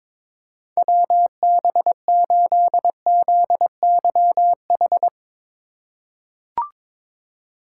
Morse code, heard as a keyed tone, about 21 wpm, 700 Hz. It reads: WB8ZYH
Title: USA Call Signs - 6 Characters - 3x Character Spacing - 22wpm